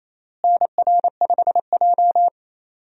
Morse code, heard as a keyed tone, about 27 wpm, 700 Hz. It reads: NR5J
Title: USA Call Signs - 4 Characters - Repeated 2x - 28wpm